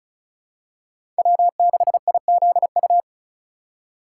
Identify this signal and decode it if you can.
Morse code, heard as a keyed tone, about 34 wpm, 700 Hz. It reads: W6IZU
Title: USA Call Signs - 5 Characters - 3x Character Spacing - 35wpm